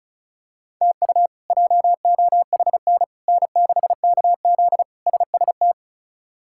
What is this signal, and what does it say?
Morse code, heard as a keyed tone, about 34 wpm, 700 Hz. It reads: TU JOHN N6KZ SST
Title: K1USN SST QSO Practice - 35wpm